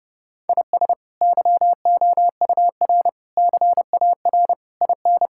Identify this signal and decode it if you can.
Morse code, heard as a keyed tone, about 30 wpm, 700 Hz. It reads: IS YOUR CAR IN
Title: Sentences from Top 1600 Words 30wpm